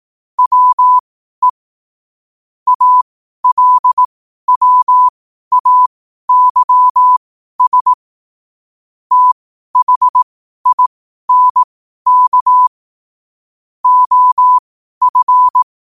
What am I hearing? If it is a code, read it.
Morse code, heard as a keyed tone, about 18 wpm, spaced as for 12 wpm, 1000 Hz. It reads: WE ALWAYS THINK OF
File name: qotdfiles / 12wpm